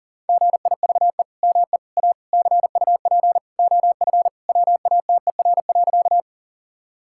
Morse code, heard as a keyed tone, about 40 wpm, 700 Hz. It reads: GIVE ME A CUP OF WATER.
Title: Sentences from Top 1600 Words 40wpm